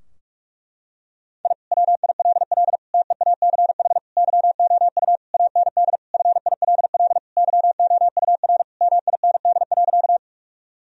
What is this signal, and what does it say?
Morse code, heard as a keyed tone, about 46 wpm, 700 Hz. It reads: I WILL TEACH YOU AND FILL YOUR MIND.
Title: Sentences from Top 1000 Words - Review Part 3 - 45wpm